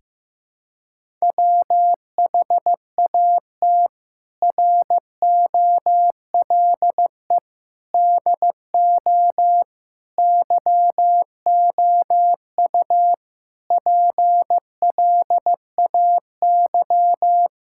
Morse code, heard as a keyed tone, about 15 wpm, 700 Hz. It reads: WHAT ROLE DO YOU PLAY
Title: Sentences from Top 1000 Words - Review Part 6 - 15wpm